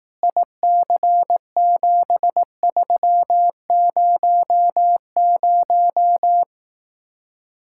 Morse code, heard as a keyed tone, about 18 wpm, 700 Hz. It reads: IC7300
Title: Speed-Racing Common QSO Phrases Encore - 15wpm